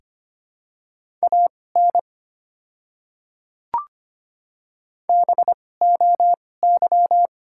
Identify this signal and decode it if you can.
Morse code, heard as a keyed tone, about 25 wpm, spaced as for 18 wpm, 690 Hz. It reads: AN BOY
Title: 2x Character Spacing - Top 300 Words - 25wpm